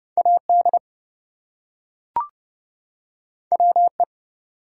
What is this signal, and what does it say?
Morse code, heard as a keyed tone, about 30 wpm, 700 Hz. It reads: AD WE
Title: Two Letter Words - 30wpm